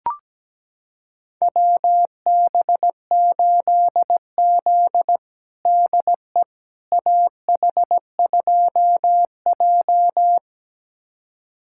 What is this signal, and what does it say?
Morse code, heard as a keyed tone, about 17 wpm, 700 Hz. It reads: WB8Z DE AH2J
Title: USA Callsign DE Callsign - 4 Characters - 17wpm